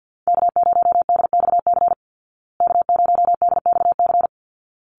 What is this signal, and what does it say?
Morse code, heard as a keyed tone, about 52 wpm, 700 Hz. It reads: K0DXC K0DXC
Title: Call Signs Repeated 3x - 50wpm